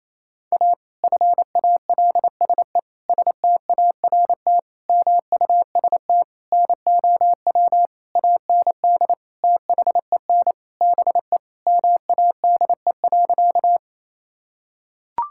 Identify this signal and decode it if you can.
Morse code, heard as a keyed tone, about 27 wpm, 700 Hz. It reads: A FALSE START MUST NOW AND THEN BE MADE.
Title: Sentences from Top 1600 Words 28wpm